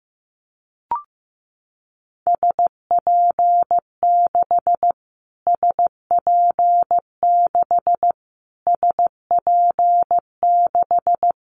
Morse code, heard as a keyed tone, about 15 wpm, 700 Hz. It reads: SP6 SP6 SP6
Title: Callsign Prefixes for Top 25 DXCC - Repeated 3x - 15wpm